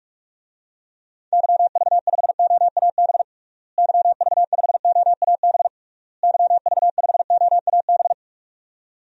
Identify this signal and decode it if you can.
Morse code, heard as a keyed tone, about 46 wpm, 700 Hz. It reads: YV5OAB YV5OAB YV5OAB
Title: DXCC-100 Call Signs - 6 Characters - Repeated 3x - 45wpm